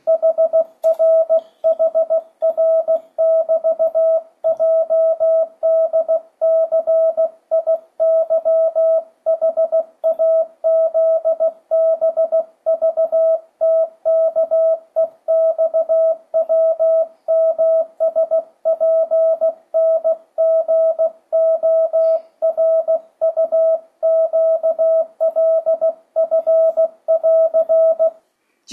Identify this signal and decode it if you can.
Morse code, heard as a keyed tone, about 16 wpm, 650 Hz, with radio noise. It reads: HRHR<BT>JDCIYHAZBVTKEXWMSPNGORUQLF<AR>
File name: 2025cw / speedup / 16wpm